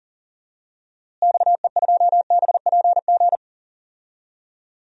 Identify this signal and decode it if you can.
Morse code, heard as a keyed tone, about 40 wpm, 700 Hz. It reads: XE2BPG
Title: DXCC-100 Call Signs - 6 Characters - Repeated 3x - 40wpm